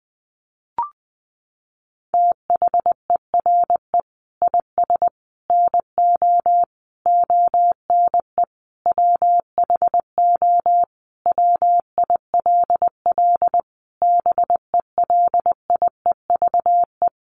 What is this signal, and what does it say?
Morse code, heard as a keyed tone, about 20 wpm, 700 Hz. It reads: THERE IS NO ONE WHO WILL BELIEVE
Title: Sentences from Top 1000 Words 20wpm